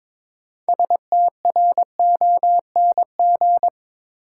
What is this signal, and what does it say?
Morse code, heard as a keyed tone, about 22 wpm, 700 Hz. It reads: STRONG
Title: Sentences from Top 500 Words - New Word Drill - 22wpm